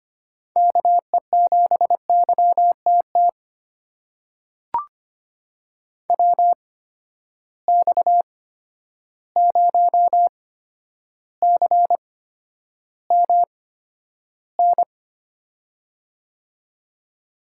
Morse code, heard as a keyed tone, about 25 wpm, spaced as for 11 wpm, 700 Hz. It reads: KE7YTT WX0CMN
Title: USA Call Signs - 6 Characters - 8x Character Spacing - 25wpm